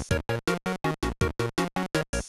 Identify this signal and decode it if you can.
There is no code.